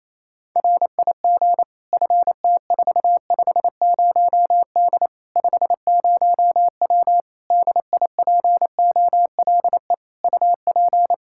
Morse code, heard as a keyed tone, about 28 wpm, 700 Hz. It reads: RIG FT450D 50W DIPOLE UP
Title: Practice QSO - Exchange 2 - Info - 28wpm